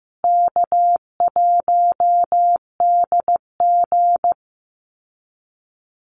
Morse code, heard as a keyed tone, about 14 wpm, 700 Hz. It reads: K1DG
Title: Call Signs Repeated 2x - 15wpm